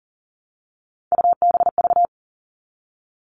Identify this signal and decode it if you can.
Morse code, heard as a keyed tone, about 40 wpm, 700 Hz. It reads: UBV
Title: All Cut Numbers - 3-Digit 40wpm